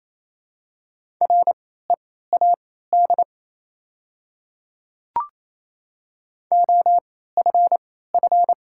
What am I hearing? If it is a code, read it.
Morse code, heard as a keyed tone, about 28 wpm, spaced as for 16 wpm, 700 Hz. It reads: READ OFF
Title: 3x Character Spacing - Top 300 Words - 28wpm